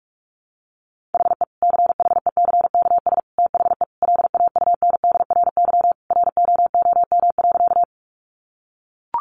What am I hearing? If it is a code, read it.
Morse code, heard as a keyed tone, about 44 wpm, 700 Hz. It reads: HE CHECKS THE LAUNDRY ROOM.
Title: Easy Phrases 45wpm